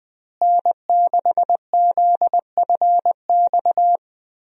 Morse code, heard as a keyed tone, about 20 wpm, 700 Hz. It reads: N6ZFX
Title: USA Call Signs - 5 Characters - Repeated 3x - 20wpm